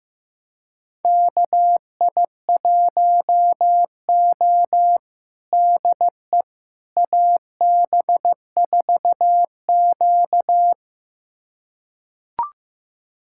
Morse code, heard as a keyed tone, about 15 wpm, 700 Hz. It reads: KI1O DE AB4Q
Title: USA Callsign DE Callsign - 4 Characters - 15wpm